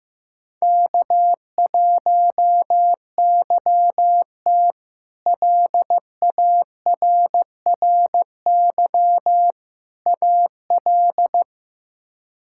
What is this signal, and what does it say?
Morse code, heard as a keyed tone, about 15 wpm, 700 Hz. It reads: K1YT LARRY AL
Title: K1USN SST QSO Practice - 15wpm